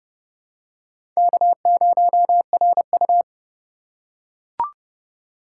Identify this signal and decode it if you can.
Morse code, heard as a keyed tone, about 30 wpm, 690 Hz. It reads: K0RU
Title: USA Call Signs - 4 Characters - 2x Character Spacing - 30wpm